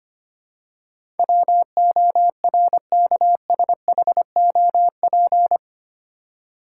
Morse code, heard as a keyed tone, about 25 wpm, 700 Hz. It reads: WORKSHOP
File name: Eight Letter Words - 25wpm